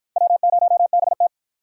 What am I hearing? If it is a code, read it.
Morse code, heard as a keyed tone, about 53 wpm, 700 Hz. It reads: W0BT